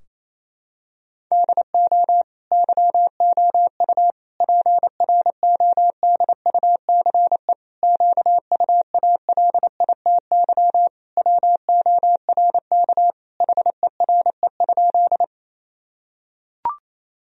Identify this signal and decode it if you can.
Morse code, heard as a keyed tone, about 28 wpm, 700 Hz. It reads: DO YOU PRODUCE QUALITY WORK HERE?
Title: Sentences from Top 1000 Words - Review Part 4 - 28wpm